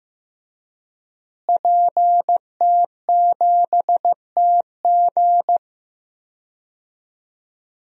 Morse code, heard as a keyed tone, about 15 wpm, 700 Hz. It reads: PT7TG
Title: DXCC-100 Call Signs - 5 Characters - Repeated 2x - 15wpm